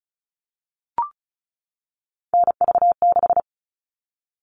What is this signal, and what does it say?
Morse code, heard as a keyed tone, about 34 wpm, 700 Hz. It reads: NV6
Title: All Cut Numbers - 3-Digit 35wpm